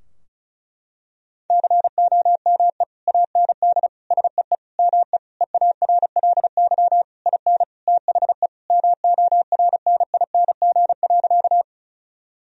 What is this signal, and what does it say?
Morse code, heard as a keyed tone, about 34 wpm, 700 Hz. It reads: COME AND SEE ME EARLY IN THE MORNING.
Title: Sentences from Top 1000 Words - Review Part 2 - 35wpm